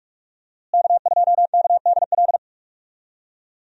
Morse code, heard as a keyed tone, about 46 wpm, 700 Hz. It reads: K2KDL
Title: USA Call Signs - 5 Characters - Repeated 2x - 45wpm